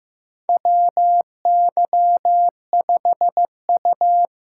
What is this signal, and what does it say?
Morse code, heard as a keyed tone, about 15 wpm, 700 Hz. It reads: WY5U
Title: USA Call Signs - 4 Characters - 2x Character Spacing - 15wpm